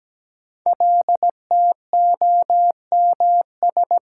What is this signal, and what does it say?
Morse code, heard as a keyed tone, about 17 wpm, 700 Hz. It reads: LTOMS
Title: Code Groups - 5 Letter:Numbers - 17wpm